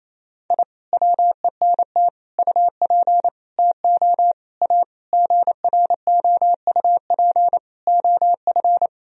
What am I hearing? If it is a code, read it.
Morse code, heard as a keyed tone, about 28 wpm, 700 Hz. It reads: I WENT UP TO A GROUP OF
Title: Sentences from Top 1000 Words - Review Part 1 - 28wpm